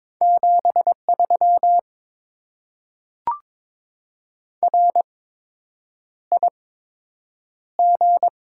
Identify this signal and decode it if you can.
Morse code, heard as a keyed tone, about 22 wpm, spaced as for 8 wpm, 700 Hz. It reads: 73 RIG
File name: Top 20 QSO Elements - 8x Character Spacing - 22wpm